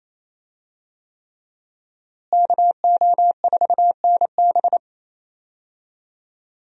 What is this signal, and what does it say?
Morse code, heard as a keyed tone, about 27 wpm, 700 Hz. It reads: KO4NB
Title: USA Call Signs - 5 Characters - Repeated 3x - 28wpm